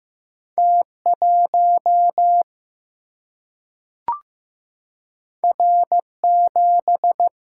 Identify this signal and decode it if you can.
Morse code, heard as a keyed tone, about 15 wpm, 700 Hz. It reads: T1 R7
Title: Single-Letter Single-Number 15wpm